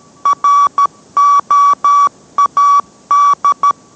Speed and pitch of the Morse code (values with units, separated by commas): 13 wpm, 1140 Hz